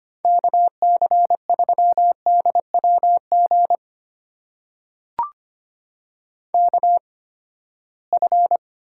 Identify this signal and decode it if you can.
Morse code, heard as a keyed tone, about 25 wpm, spaced as for 17 wpm, 700 Hz. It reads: KC3DWG KF
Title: USA Call Signs - 8x Character Spacing - 25wpm